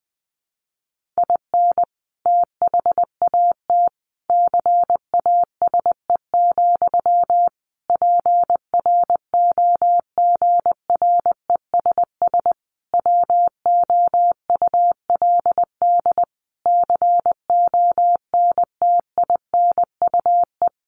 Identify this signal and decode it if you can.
Morse code, heard as a keyed tone, about 20 wpm, 700 Hz. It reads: IN THAT CASE, PROGRESS WOULD CONTINUE
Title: Sentences from Top 1400 Words 20wpm